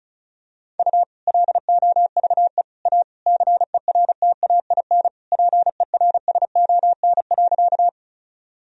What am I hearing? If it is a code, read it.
Morse code, heard as a keyed tone, about 36 wpm, 700 Hz. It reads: U LOVE A CERTAIN PERSON.
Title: Sentences from Top 500 Words and Abbreviations 35wpm